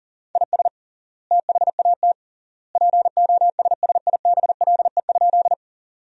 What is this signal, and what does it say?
Morse code, heard as a keyed tone, about 40 wpm, 700 Hz. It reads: IS THAT POSSIBLE?
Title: 3x Word-Spacing - Sentences from Top 500 Words - 40wpm